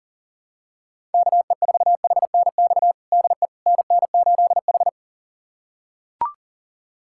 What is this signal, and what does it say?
Morse code, heard as a keyed tone, about 40 wpm, 700 Hz. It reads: KE4HNX DE NN8H
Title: USA Callsign DE Callsign 40wpm